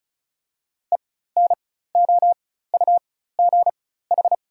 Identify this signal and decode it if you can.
Morse code, heard as a keyed tone, about 36 wpm, spaced as for 16 wpm, 700 Hz. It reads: ENOUGH
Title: 4x Character Spacing - Top 500 Words - 35wpm